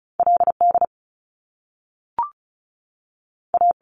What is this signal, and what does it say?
Morse code, heard as a keyed tone, about 36 wpm, 700 Hz. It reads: LD A
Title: Top Two Letter Combinations in Words 35wpm